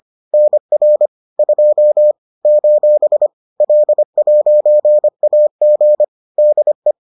Morse code, heard as a keyed tone, about 25 wpm, 600 Hz. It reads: NR 2 : L'AG DE
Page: bulletin UFT sous forme texte et en CW